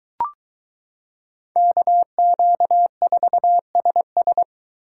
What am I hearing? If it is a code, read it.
Morse code, heard as a keyed tone, about 22 wpm, 700 Hz. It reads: KQ4SS